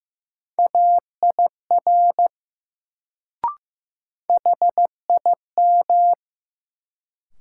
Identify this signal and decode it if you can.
Morse code, heard as a keyed tone, about 15 wpm, 700 Hz. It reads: AIR HIM
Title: Sets of 2 Words - New Word Drill - 15wpm